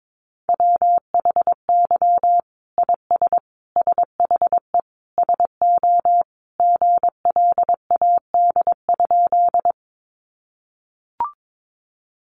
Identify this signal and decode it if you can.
Morse code, heard as a keyed tone, about 22 wpm, 700 Hz. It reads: WHY IS SHE SO GLAD?
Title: Sentences from Top 800 Words 22wpm